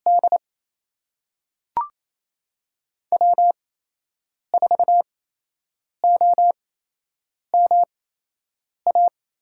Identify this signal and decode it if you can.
Morse code, heard as a keyed tone, about 28 wpm, spaced as for 8 wpm, 700 Hz. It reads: D W4OMA
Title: USA Call Signs - 5 Characters - 8x Character Spacing - 28wpm